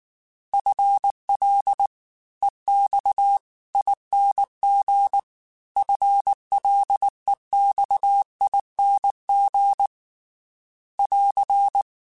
Morse code, heard as a keyed tone, about 19 wpm, 800 Hz. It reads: FL EX ING FLEXING <AR>
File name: FL / FL mp3s / Fast FL @18